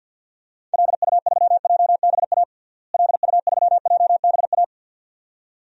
Morse code, heard as a keyed tone, about 53 wpm, 700 Hz. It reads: LU3JBA LU3JBA
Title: DXCC-100 Call Signs - 6 Characters - Repeated 2x - 50wpm